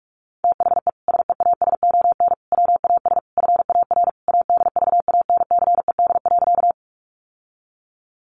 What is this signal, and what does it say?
Morse code, heard as a keyed tone, about 44 wpm, 700 Hz. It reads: THE SEASON WAS FAR ADVANCED.
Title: Sentences from Top 1400 Words 45wpm